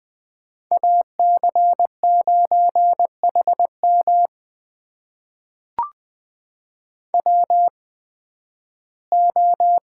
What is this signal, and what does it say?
Morse code, heard as a keyed tone, about 20 wpm, spaced as for 13 wpm, 700 Hz. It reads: AC9HM WO
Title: USA Call Signs - 5 Characters - 8x Character Spacing - 20wpm